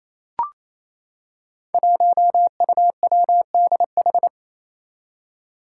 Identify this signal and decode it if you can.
Morse code, heard as a keyed tone, about 28 wpm, 700 Hz. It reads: 1UWDH
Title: Five Challenging Letter-Number Combinations 28wpm